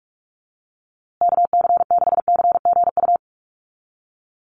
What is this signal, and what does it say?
Morse code, heard as a keyed tone, about 44 wpm, 690 Hz. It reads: KC6CGU
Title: USA Call Signs Repeated 2x 45wpm